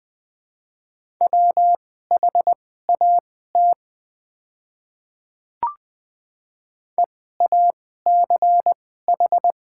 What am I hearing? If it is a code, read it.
Morse code, heard as a keyed tone, about 20 wpm, spaced as for 15 wpm, 700 Hz. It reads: WHAT EACH
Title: 2x Character Spacing - Top 100 Words - 20wpm